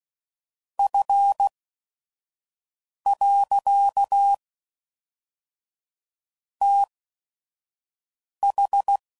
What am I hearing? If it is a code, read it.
Morse code, heard as a keyed tone, about 16 wpm, 800 Hz. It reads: F . T H